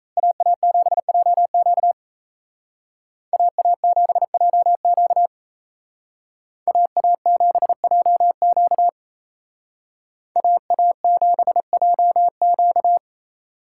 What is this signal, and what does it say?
Morse code, heard as a keyed tone, about 32 wpm, 700 Hz. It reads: AA7JQ AA7JQ AA7JQ AA7JQ
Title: Speed Racing - USA 2x2 Call Signs 28wpm